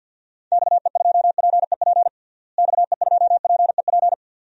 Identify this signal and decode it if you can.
Morse code, heard as a keyed tone, about 49 wpm, 700 Hz. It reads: XE2PEP XE2PEP
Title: DXCC-100 Call Signs - 6 Characters - Repeated 2x - 50wpm